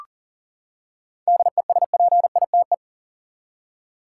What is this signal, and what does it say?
Morse code, heard as a keyed tone, about 40 wpm, 700 Hz. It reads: DESPITE
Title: Seven Letter Words - 40wpm